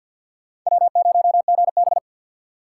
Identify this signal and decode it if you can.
Morse code, heard as a keyed tone, about 47 wpm, 700 Hz. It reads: W0GB
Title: USA Call Signs - 2x Character Spacing - 50wpm